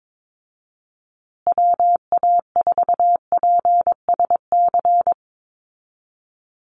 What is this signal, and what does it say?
Morse code, heard as a keyed tone, about 22 wpm, 700 Hz. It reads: WA4PSC
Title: Call Signs Repeated 2x - 22wpm